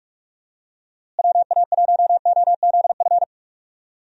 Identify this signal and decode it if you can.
Morse code, heard as a keyed tone, about 48 wpm, 700 Hz. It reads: WA1OZF
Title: USA Call Signs - 6 Characters - Repeated 3x - 45wpm